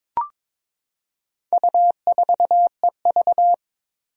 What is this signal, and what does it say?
Morse code, heard as a keyed tone, about 22 wpm, 700 Hz. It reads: U4EV
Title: Four Challenging Letter-Number Combinations 22wpm